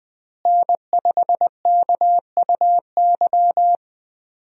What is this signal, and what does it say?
Morse code, heard as a keyed tone, about 20 wpm, 700 Hz. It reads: N5KUY
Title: USA Call Signs - 5 Characters - Repeated 3x - 20wpm